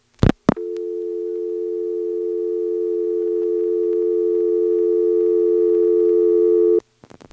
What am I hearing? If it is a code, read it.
Dial tone (North America), 350 + 440 Hz.